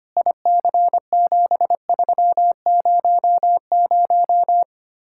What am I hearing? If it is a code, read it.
Morse code, heard as a keyed tone, about 31 wpm, 700 Hz. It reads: IC7300
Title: Speed-Racing Common QSO Phrases - 25wpm